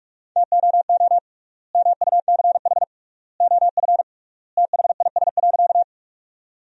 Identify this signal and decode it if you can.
Morse code, heard as a keyed tone, about 46 wpm, 700 Hz. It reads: TOO MUCH OF THIS.
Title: 3x Word-Spacing - Sentences from Top 200 Words - 45wpm